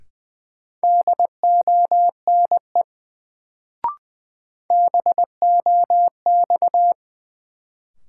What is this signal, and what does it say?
Morse code, heard as a keyed tone, about 20 wpm, 700 Hz. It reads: DONE BOX
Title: Sentences from Top 500 Words - New Word Drill - 20wpm